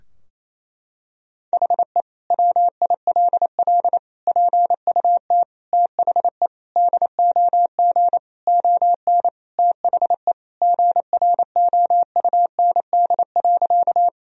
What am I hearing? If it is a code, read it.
Morse code, heard as a keyed tone, about 28 wpm, 700 Hz. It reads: HE WILL PUT THE DOG ON THE GROUND.
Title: Sentences from Top 1000 Words - Review Part 2 - 28wpm